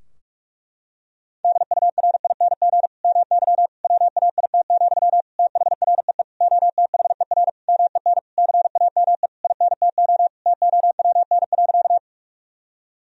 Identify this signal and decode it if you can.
Morse code, heard as a keyed tone, about 46 wpm, 700 Hz. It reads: DURING MY WAIT, THREE OTHER MEN CAME INTO TOWN.
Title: Sentences from Top 1000 Words - Review Part 2 - 45wpm